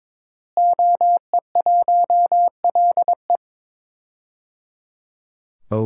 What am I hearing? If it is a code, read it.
Morse code, heard as a keyed tone, about 21 wpm, 700 Hz. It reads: OE1LE
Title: DXCC-100 Call Signs - 5 Characters - Repeated 3x - 22wpm